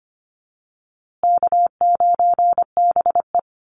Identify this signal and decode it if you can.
Morse code, heard as a keyed tone, about 25 wpm, 700 Hz. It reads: K9BE
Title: USA Call Signs Repeated 2x 25wpm